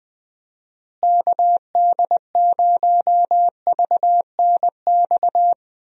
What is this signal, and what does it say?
Morse code, heard as a keyed tone, about 21 wpm, 700 Hz. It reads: KD0VNX
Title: USA Call Signs - 6 Characters - Repeated 2x - 20wpm